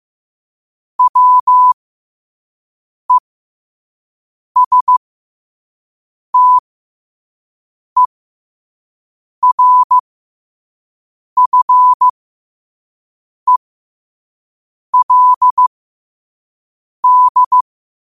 Morse code, heard as a keyed tone, about 15 wpm, spaced as for 5 wpm, 1000 Hz. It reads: WESTERFELD
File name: qotdfiles / 5wpm